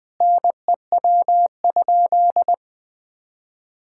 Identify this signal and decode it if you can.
Morse code, heard as a keyed tone, about 21 wpm, 700 Hz. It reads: NEW?
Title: Sentences with 6-Word Phrases - 20wpm